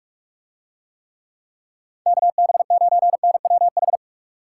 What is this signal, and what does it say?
Morse code, heard as a keyed tone, about 44 wpm, 700 Hz. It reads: KB9NWH